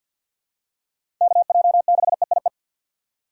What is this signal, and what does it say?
Morse code, heard as a keyed tone, about 51 wpm, 700 Hz. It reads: KJ6EIE
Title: USA Call Signs - 2x Character Spacing - 50wpm